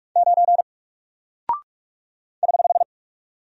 Morse code, heard as a keyed tone, about 44 wpm, 700 Hz. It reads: 9 <HH>
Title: Single Character (Letter:Number:Punctuation:Prosign) 45wpm